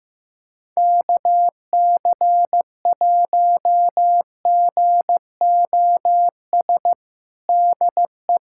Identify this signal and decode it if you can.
Morse code, heard as a keyed tone, about 15 wpm, 700 Hz. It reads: KC1GOS DE
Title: USA Callsign DE Callsign - 6 Characters - 15wpm